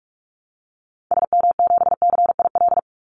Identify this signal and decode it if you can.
Morse code, heard as a keyed tone, about 44 wpm, 700 Hz. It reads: SM7CIL
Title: Call Signs Repeated 2x - 45wpm